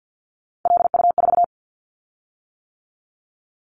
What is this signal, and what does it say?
Morse code, heard as a keyed tone, about 50 wpm, 700 Hz. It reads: LU4
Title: Callsign Prefixes for Top 100 DXCC - 50wpm